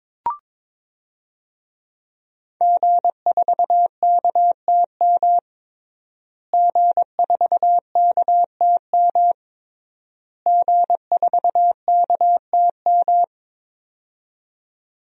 Morse code, heard as a keyed tone, about 22 wpm, 700 Hz. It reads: G4KTM G4KTM G4KTM
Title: DXCC-100 Call Signs - 5 Characters - Repeated 3x - 22wpm